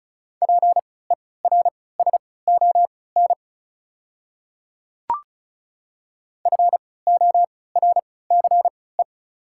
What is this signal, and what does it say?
Morse code, heard as a keyed tone, about 36 wpm, spaced as for 20 wpm, 690 Hz. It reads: PERSON FORCE
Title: 3x Character Spacing - Top 500 Words - 35wpm